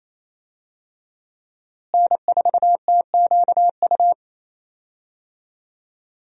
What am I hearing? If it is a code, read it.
Morse code, heard as a keyed tone, about 28 wpm, 700 Hz. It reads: N4TQU
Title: USA Call Signs - 5 Characters - Repeated 2x - 28wpm